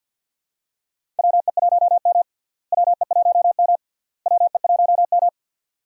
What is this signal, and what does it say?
Morse code, heard as a keyed tone, about 46 wpm, 700 Hz. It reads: WE1M WE1M WE1M
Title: USA Call Signs - 4 Characters - Repeated 3x - 50wpm